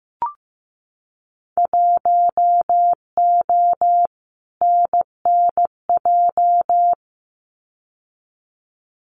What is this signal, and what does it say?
Morse code, heard as a keyed tone, about 15 wpm, 700 Hz. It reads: 1O NNJ
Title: Winter Field Day Exchanges 15wpm